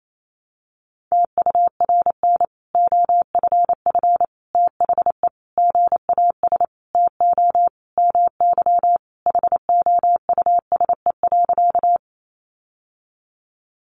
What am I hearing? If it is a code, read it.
Morse code, heard as a keyed tone, about 28 wpm, 700 Hz. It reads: TURN OFF THE GAS TO MY HOUSE.
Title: Sentences from Top 1100 Words 28wpm